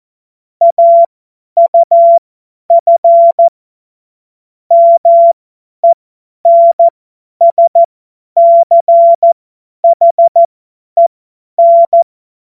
Morse code, heard as a keyed tone, about 14 wpm, spaced as for 10 wpm, 680 Hz. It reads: AUF MENSCHEN
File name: Morsekurs / Witze / Witz 02. August 2022 50 F70